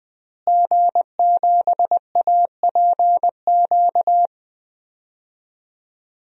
Morse code, heard as a keyed tone, about 20 wpm, 700 Hz. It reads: G7APQ
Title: DXCC-100 Call Signs - 5 Characters - Repeated 3x - 20wpm